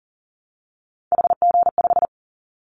Morse code, heard as a keyed tone, about 40 wpm, 700 Hz. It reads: HG5
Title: Callsign Prefixes for Top 25 DXCC - Repeated 3x - 40wpm